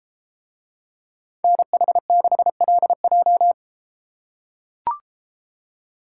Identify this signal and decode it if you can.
Morse code, heard as a keyed tone, about 32 wpm, 700 Hz. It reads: NH6LJ